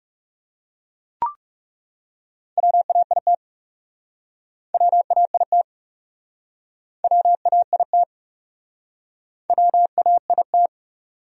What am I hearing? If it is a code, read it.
Morse code, heard as a keyed tone, about 40 wpm, 700 Hz. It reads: WAIT WAIT WAIT WAIT